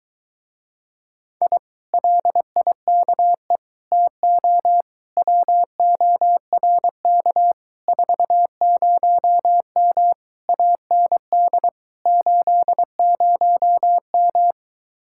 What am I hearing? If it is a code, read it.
Morse code, heard as a keyed tone, about 22 wpm, 700 Hz. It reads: I LIKE TO WORK 40M AND 80M